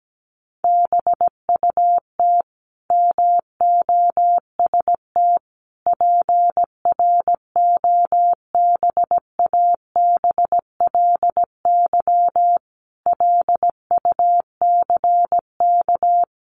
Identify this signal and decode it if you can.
Morse code, heard as a keyed tone, about 17 wpm, 700 Hz. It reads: BUT MOST PROBABLY LUCK
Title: Sentences from Top 1500 Words 17wpm